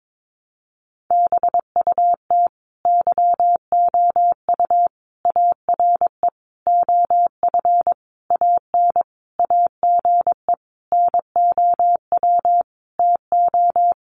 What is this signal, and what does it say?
Morse code, heard as a keyed tone, about 22 wpm, 700 Hz. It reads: BUT YOU ARE OF AN AGE NOW TO
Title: Sentences from Top 1300 Words 22wpm